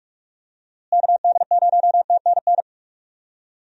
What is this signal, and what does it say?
Morse code, heard as a keyed tone, about 46 wpm, 700 Hz. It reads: KD0TNN
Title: USA Call Signs - 6 Characters - Repeated 3x - 45wpm